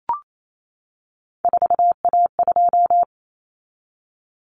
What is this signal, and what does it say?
Morse code, heard as a keyed tone, about 28 wpm, 700 Hz. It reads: 4A2
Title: Common Cut Numbers - 3-Digit 28wpm